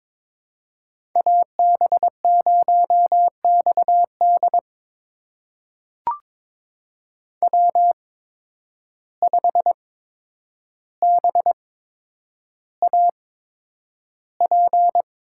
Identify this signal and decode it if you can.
Morse code, heard as a keyed tone, about 22 wpm, spaced as for 9 wpm, 700 Hz. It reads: AB0XD W5BAP
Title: USA Call Signs - 5 Characters - 8x Character Spacing - 22wpm